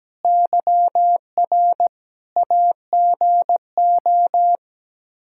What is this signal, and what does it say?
Morse code, heard as a keyed tone, about 17 wpm, 700 Hz. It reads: YR AGO
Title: Practice QSO - Exchange 3 - More Info - 17wpm